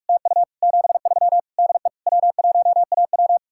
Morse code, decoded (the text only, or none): TU 73 DE W1AW